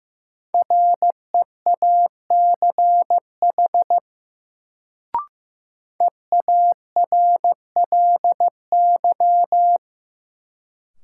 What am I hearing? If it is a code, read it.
Morse code, heard as a keyed tone, about 15 wpm, 700 Hz. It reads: REACH EARLY
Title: Sentences from Top 400 Words - New Word Drill - 15wpm